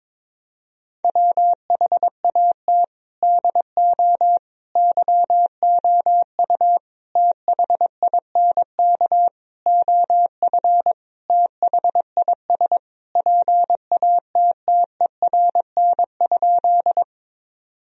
Morse code, heard as a keyed tone, about 22 wpm, 700 Hz. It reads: WHAT DO YOU THINK OF THIS PATTERN?
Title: Sentences from Top 1000 Words - Review Part 2 - 22wpm